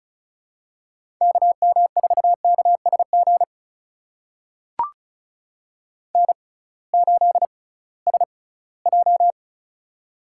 Morse code, heard as a keyed tone, about 36 wpm, spaced as for 20 wpm, 700 Hz. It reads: KM4KSG N8SJ
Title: USA Call Signs - 6x Character Spacing - 35wpm